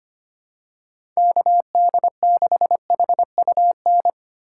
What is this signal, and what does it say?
Morse code, heard as a keyed tone, about 25 wpm, 700 Hz. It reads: KD6HUN
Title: USA Call Signs - 6 Characters - 4x Character Spacing - 25wpm